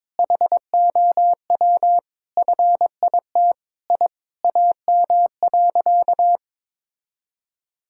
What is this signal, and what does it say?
Morse code, heard as a keyed tone, about 22 wpm, 700 Hz. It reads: HOW FIT I AM.
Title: Sentences from Top 1000 Words - Review Part 5 - 22wpm